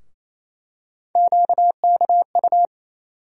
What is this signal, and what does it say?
Morse code, heard as a keyed tone, about 28 wpm, 700 Hz. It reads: QKU
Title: USA Call Sign Postfixes - 3 Letters 28wpm